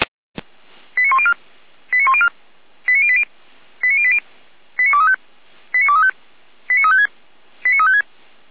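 Five-tone selective call (ZVEI), heard as quick runs of five tones, tones of about 70 ms. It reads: 89104 89104 89080 89080 89235 89235 89356 89356
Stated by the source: ZVEI five-tone selective call received by SDR, served from audio.f85360.feuerwehr-orlamuende.org (Feuerwehr Orlamünde).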